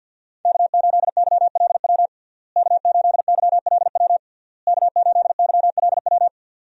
Morse code, heard as a keyed tone, about 52 wpm, 700 Hz. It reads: K8YLW K8YLW K8YLW
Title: USA Call Signs - 5 Characters - Repeated 3x - 50wpm